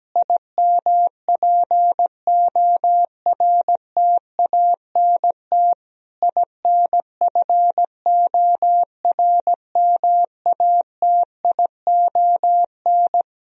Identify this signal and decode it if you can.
Morse code, heard as a keyed tone, about 17 wpm, 700 Hz. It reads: IMPORTANT INFORMATION
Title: Sentences from Top 1000 Words - Review Part 3 - 17wpm